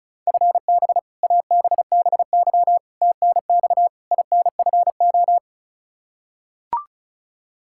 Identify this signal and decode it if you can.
Morse code, heard as a keyed tone, about 34 wpm, 700 Hz. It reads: FB ABBY TNX INFO
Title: Practice QSO - Exchange 3 - More Info - 35wpm